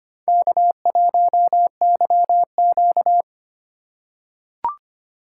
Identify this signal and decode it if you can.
Morse code, heard as a keyed tone, about 25 wpm, 700 Hz. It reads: K1YQ
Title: USA Call Signs - 4 Characters - 2x Character Spacing - 25wpm